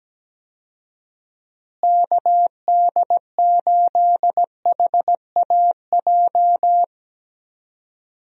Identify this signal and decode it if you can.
Morse code, heard as a keyed tone, about 17 wpm, 700 Hz. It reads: KD8HAJ
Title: USA Call Signs - 6 Characters - Repeated 3x - 17wpm